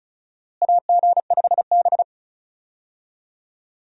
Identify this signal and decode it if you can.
Morse code, heard as a keyed tone, about 34 wpm, 700 Hz. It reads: AG5B
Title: USA Call Signs - 4 Characters - Repeated 2x - 35wpm